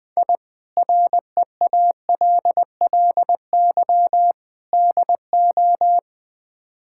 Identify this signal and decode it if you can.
Morse code, heard as a keyed tone, about 20 wpm, 700 Hz. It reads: I REALLY DO
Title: Build A Phrase - 4-Word Phrases - 20wpm